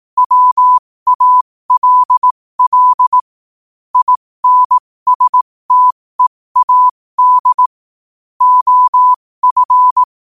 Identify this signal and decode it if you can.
Morse code, heard as a keyed tone, about 18 wpm, spaced as for 15 wpm, 1000 Hz. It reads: WALL INSTEAD OF
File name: qotdfiles / 15wpm